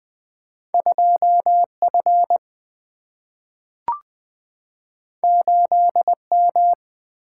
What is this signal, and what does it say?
Morse code, heard as a keyed tone, about 20 wpm, 700 Hz. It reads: 2F 8M
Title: Single-Letter Single-Number 20wpm